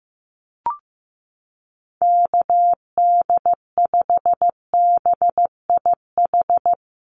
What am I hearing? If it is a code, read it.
Morse code, heard as a keyed tone, about 15 wpm, 700 Hz. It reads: KD5BIH
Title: USA Call Signs Repeated 3x 15wpm